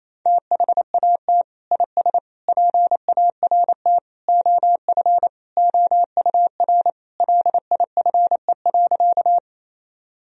Morse code, heard as a keyed tone, about 28 wpm, 700 Hz. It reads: THAT IS PART OF OUR LIFE.
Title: Sentences from Top 1000 Words - Review Part 1 - 28wpm